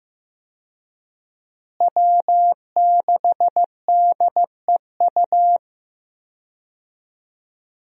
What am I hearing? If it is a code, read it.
Morse code, heard as a keyed tone, about 15 wpm, 700 Hz. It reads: W6DEU
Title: USA Call Signs - 5 Characters - Repeated 3x - 15wpm